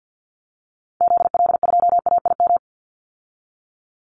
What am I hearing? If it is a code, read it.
Morse code, heard as a keyed tone, about 49 wpm, 700 Hz. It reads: ZL2AIM
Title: Call Signs Repeated 3x - 50wpm